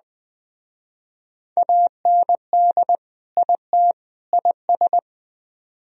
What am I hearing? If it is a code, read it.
Morse code, heard as a keyed tone, about 20 wpm, 700 Hz. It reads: AND IT IS